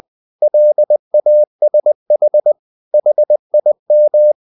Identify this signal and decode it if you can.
Morse code, heard as a keyed tone, about 20 wpm, 600 Hz. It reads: LASH HIM
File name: the old man and the sea part 2 20wpm